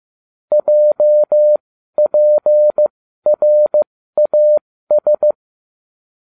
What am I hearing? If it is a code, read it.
Morse code, heard as a keyed tone, about 15 wpm, spaced as for 12 wpm, 600 Hz. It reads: JPRAS